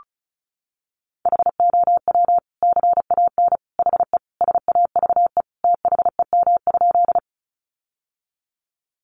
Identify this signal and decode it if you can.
Morse code, heard as a keyed tone, about 34 wpm, 700 Hz. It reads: HOW CAN HE SAVE THEM?
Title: Sentences from Top 1100 Words 35wpm